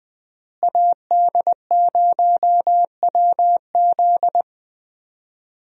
Morse code, heard as a keyed tone, about 21 wpm, 700 Hz. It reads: AD0WZ
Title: USA Call Signs - 5 Characters - Repeated 2x - 20wpm